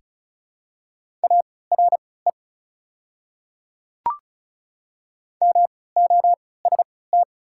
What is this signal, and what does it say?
Morse code, heard as a keyed tone, about 36 wpm, spaced as for 20 wpm, 690 Hz. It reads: ARE MOST
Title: 3x Character Spacing - Top 100 Words - 35wpm